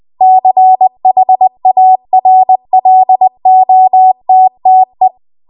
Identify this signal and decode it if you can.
Morse code, heard as a keyed tone, about 20 wpm, 750 Hz. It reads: CHARLOTTE